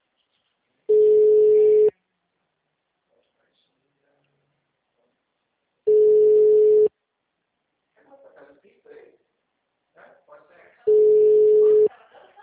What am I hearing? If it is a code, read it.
Ringback tone (Europe, 425 Hz), with speech.